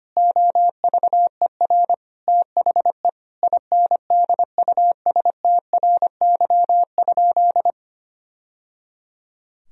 Morse code, heard as a keyed tone, about 25 wpm, 700 Hz. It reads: OVER THE INDUSTRY?
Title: Sentences from Top 1000 Words - Review Part 4 - 25wpm